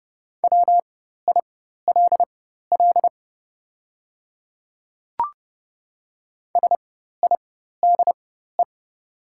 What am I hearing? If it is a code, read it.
Morse code, heard as a keyed tone, about 30 wpm, spaced as for 14 wpm, 700 Hz. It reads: WILL SIDE
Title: 4x Character Spacing - Top 100 Words - 30wpm